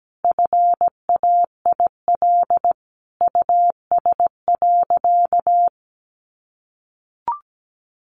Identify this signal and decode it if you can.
Morse code, heard as a keyed tone, about 17 wpm, 700 Hz. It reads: FAIL US.
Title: Sentences from Top 1100 Words 17wpm